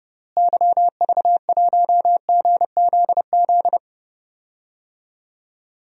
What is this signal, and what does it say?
Morse code, heard as a keyed tone, about 31 wpm, 700 Hz. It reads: YV1GZZ
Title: DXCC-100 Call Signs - 6 Characters - Repeated 2x - 30wpm